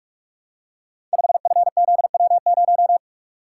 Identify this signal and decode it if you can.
Morse code, heard as a keyed tone, about 44 wpm, 700 Hz. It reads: 5FZW0
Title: Code Groups - 5 Letter:Numbers - 45wpm